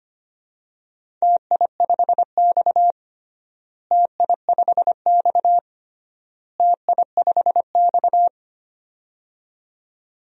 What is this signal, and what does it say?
Morse code, heard as a keyed tone, about 25 wpm, 700 Hz. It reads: TI5X TI5X TI5X
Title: DXCC-100 Call Signs - 4 Characters - Repeated 3x - 25wpm